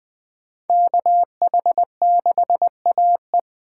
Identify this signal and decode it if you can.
Morse code, heard as a keyed tone, about 21 wpm, 700 Hz. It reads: KH6AE